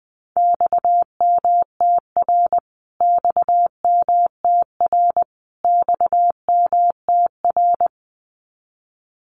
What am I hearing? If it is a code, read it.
Morse code, heard as a keyed tone, about 20 wpm, 700 Hz. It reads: XMTR XMTR XMTR
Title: Abbreviations Repeated 3x - 20wpm